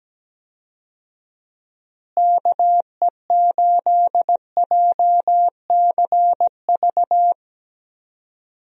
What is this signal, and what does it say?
Morse code, heard as a keyed tone, about 17 wpm, 700 Hz. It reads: KE8JCV
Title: USA Call Signs - 6 Characters - Repeated 2x - 17wpm